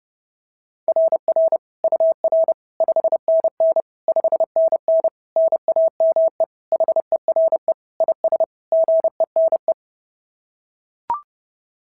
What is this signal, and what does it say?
Morse code, heard as a keyed tone, about 30 wpm, 690 Hz. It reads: RR UR 5NN 5NN NAME HERE IS GENE
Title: SOTA QSO Practice 30wpm